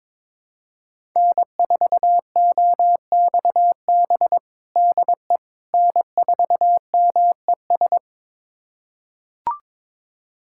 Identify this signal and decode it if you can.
Morse code, heard as a keyed tone, about 22 wpm, 700 Hz. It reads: N4OXB DE N4MES
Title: USA Callsign DE Callsign - 5 Characters - 22wpm